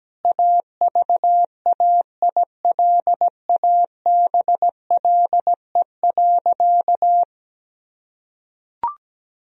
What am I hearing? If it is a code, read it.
Morse code, heard as a keyed tone, about 17 wpm, 700 Hz. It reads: AVAILABLE.
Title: Sentences from Top 1000 Words - Review Part 3 - 17wpm